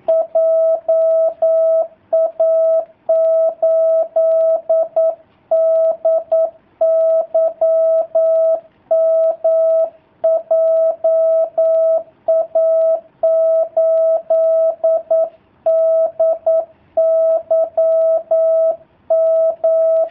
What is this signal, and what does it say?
Morse code, heard as a keyed tone, about 9 wpm, 650 Hz. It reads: JA8DYM JA8DYM
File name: jh8cbh / 2019cw / sp09- w